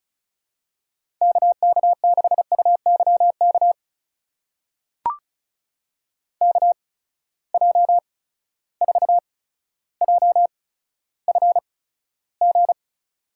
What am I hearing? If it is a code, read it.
Morse code, heard as a keyed tone, about 36 wpm, spaced as for 15 wpm, 700 Hz. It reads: KK6UYK KJ4JFG
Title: USA Call Signs - 6 Characters - 8x Character Spacing - 35wpm